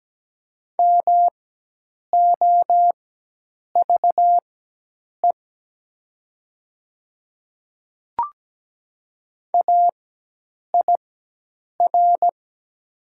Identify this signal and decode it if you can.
Morse code, heard as a keyed tone, about 17 wpm, spaced as for 8 wpm, 700 Hz. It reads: MOVE AIR
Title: 4x Character Spacing - Top 200 Words - 17wpm